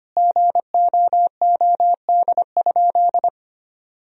Morse code, heard as a keyed tone, about 25 wpm, 700 Hz. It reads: GOOD?
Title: Sentences from Top 1000 Words - Review Part 3 - 25wpm